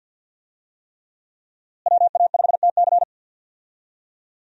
Morse code, heard as a keyed tone, about 46 wpm, 700 Hz. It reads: WA5TC
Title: USA Call Signs - 5 Characters - Repeated 3x - 50wpm